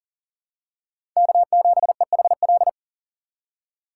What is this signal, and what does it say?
Morse code, heard as a keyed tone, about 40 wpm, 700 Hz. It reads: K7EHL
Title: USA Call Signs - 4x Character Spacing - 40wpm